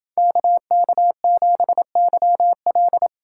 Morse code, heard as a keyed tone, about 27 wpm, 700 Hz. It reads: KK7YL